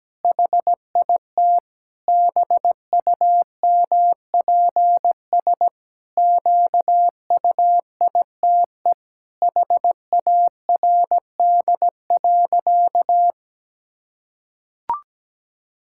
Morse code, heard as a keyed tone, about 17 wpm, 700 Hz. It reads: HIT BUMPS QUITE HARD.
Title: Pangram Challenge v2 - 17wpm